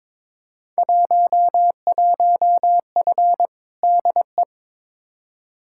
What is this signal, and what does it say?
Morse code, heard as a keyed tone, about 22 wpm, 700 Hz. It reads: 11F DE
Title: Field Day Exchanges Encore - 22wpm